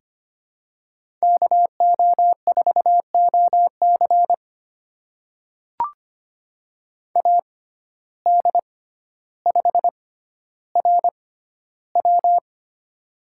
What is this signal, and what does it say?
Morse code, heard as a keyed tone, about 25 wpm, spaced as for 13 wpm, 700 Hz. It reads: KO4OC AD5RW
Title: USA Call Signs - 6x Character Spacing - 25wpm